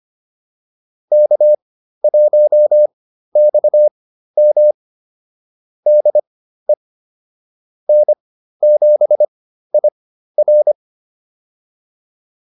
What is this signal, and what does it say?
Morse code, heard as a keyed tone, about 25 wpm, spaced as for 13 wpm, 600 Hz. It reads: K1XM DE N7IR